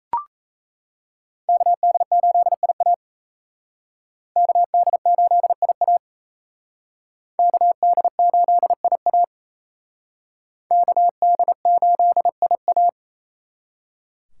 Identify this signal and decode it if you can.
Morse code, heard as a keyed tone, about 36 wpm, 700 Hz. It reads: KD8IA KD8IA KD8IA KD8IA